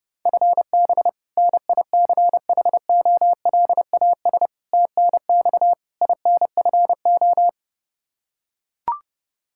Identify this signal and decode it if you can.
Morse code, heard as a keyed tone, about 32 wpm, 700 Hz. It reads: FB NICHOLAS TNX INFO
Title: K1USN SST QSO Practice 30wpm